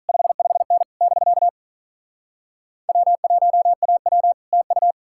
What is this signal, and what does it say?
Morse code, heard as a keyed tone, about 46 wpm, 700 Hz. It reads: 55N <BK> W1AW TU